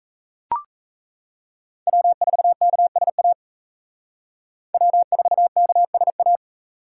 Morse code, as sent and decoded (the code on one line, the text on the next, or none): .-- ....- -.- ... .- / .-- ....- -.- ... .-
W4KSA W4KSA